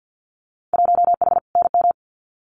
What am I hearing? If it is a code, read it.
Morse code, heard as a keyed tone, about 48 wpm, 700 Hz. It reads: 1H NM
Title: Winter Field Day Exchanges 50wpm